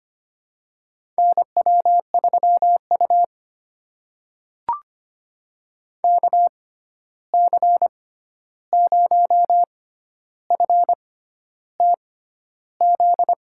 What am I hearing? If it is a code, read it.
Morse code, heard as a keyed tone, about 25 wpm, spaced as for 11 wpm, 700 Hz. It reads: NW3U KC0FTZ
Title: USA Call Signs - 6x Character Spacing - 25wpm